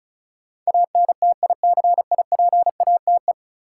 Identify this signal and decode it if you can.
Morse code, heard as a keyed tone, about 36 wpm, 700 Hz. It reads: ANTICIPATE
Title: Sentences from Top 1500 Words - New Word Drill - 35wpm